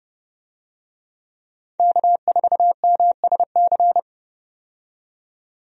Morse code, heard as a keyed tone, about 30 wpm, 700 Hz. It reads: K4MSC
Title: USA Call Signs - 5 Characters - 30wpm